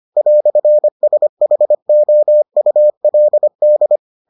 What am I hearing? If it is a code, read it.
Morse code, heard as a keyed tone, about 25 wpm, 600 Hz. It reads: "SHOULD